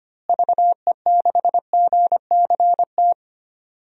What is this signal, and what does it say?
Morse code, heard as a keyed tone, about 25 wpm, 700 Hz. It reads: VE6GCT
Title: DXCC-100 Call Signs - 6 Characters - Repeated 2x - 25wpm